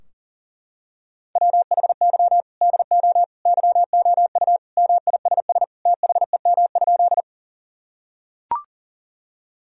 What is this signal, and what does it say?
Morse code, heard as a keyed tone, about 40 wpm, 700 Hz. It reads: WHY DO YOU MISS THEM?
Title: Sentences from Top 1000 Words - Review Part 3 - 40wpm